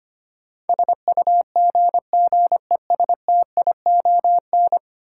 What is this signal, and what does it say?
Morse code, heard as a keyed tone, about 25 wpm, 700 Hz. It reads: SUGGESTION
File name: 10 Letter Words - First 3 Letters Given - 25wpm